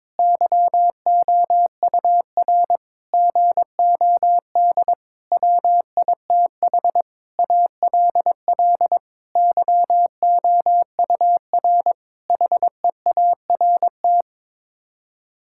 Morse code, heard as a keyed tone, about 22 wpm, 700 Hz. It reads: YOUR GOD WITH ALL YOUR HEART
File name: Sets of 9 Words - 22wpm